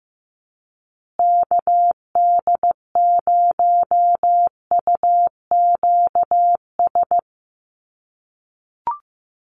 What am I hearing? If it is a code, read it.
Morse code, heard as a keyed tone, about 15 wpm, 700 Hz. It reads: KD0UQS
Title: USA Call Signs Repeated 3x 15wpm